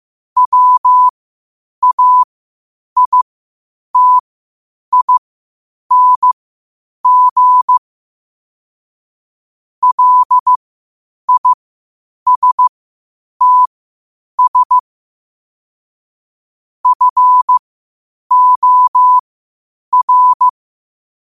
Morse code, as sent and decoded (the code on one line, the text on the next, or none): .-- .- .. - .. -. --. / .-.. .. ... - ... / ..-. --- .-.
WAITING LISTS FOR